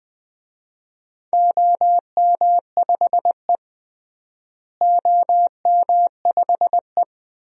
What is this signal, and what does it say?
Morse code, heard as a keyed tone, about 20 wpm, 700 Hz. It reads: OM5E OM5E
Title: DXCC-100 Call Signs - 4 Characters - Repeated 2x - 20wpm